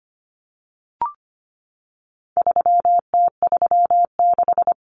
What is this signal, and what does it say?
Morse code, heard as a keyed tone, about 25 wpm, 700 Hz. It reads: 3T36
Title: Common Cut Numbers - 4-Digit 25wpm